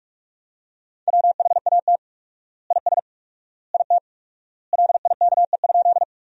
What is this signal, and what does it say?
Morse code, heard as a keyed tone, about 46 wpm, 700 Hz. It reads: WHAT IS IT LIKE?
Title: 4x Word-Spacing - Sentences from Top 100 Words - 45wpm